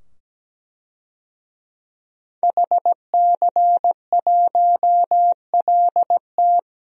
Morse code, heard as a keyed tone, about 17 wpm, 700 Hz. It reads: HC1LT
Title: DXCC-100 Call Signs - 5 Characters - 17wpm